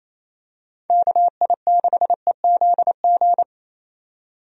Morse code, heard as a keyed tone, about 28 wpm, 700 Hz. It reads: KI6EZG
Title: USA Call Signs - 6 Characters - 28wpm